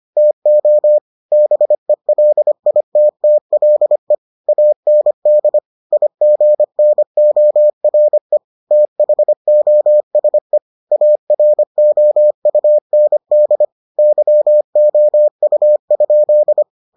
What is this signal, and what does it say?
Morse code, heard as a keyed tone, about 25 wpm, 600 Hz. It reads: TO BELITTLE AND IGNORE THOSE AROUND YOU?